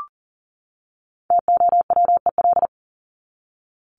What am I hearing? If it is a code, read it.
Morse code, heard as a keyed tone, about 40 wpm, 700 Hz. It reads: TOWEL
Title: Top 1501-1750 Words 40wpm